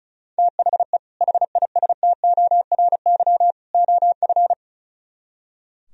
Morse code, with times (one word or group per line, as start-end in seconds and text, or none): 0.4-1.0 THE
1.2-3.5 HISTORY
3.7-4.5 OF